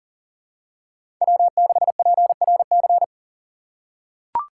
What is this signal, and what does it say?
Morse code, heard as a keyed tone, about 37 wpm, 700 Hz. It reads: W6PRC
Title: USA Call Signs - 6x Character Spacing - 40wpm